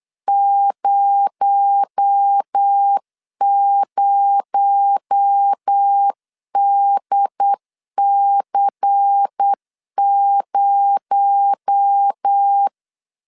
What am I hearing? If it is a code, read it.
Morse code, heard as a keyed tone, about 8 wpm, 800 Hz. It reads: 00DC0
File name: LEZIONI / 6 PKX0 / 6 PKX0 8wpm 3341